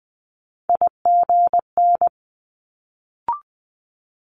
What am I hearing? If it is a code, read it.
Morse code, heard as a keyed tone, about 20 wpm, 700 Hz. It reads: IGN
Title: Multi-letter Phonemes Quick Fire 20wpm